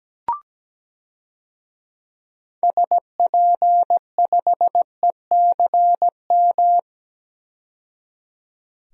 Morse code, heard as a keyed tone, about 17 wpm, 700 Hz. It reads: SP5ECM
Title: DXCC-100 Call Signs - 6 Characters - 17wpm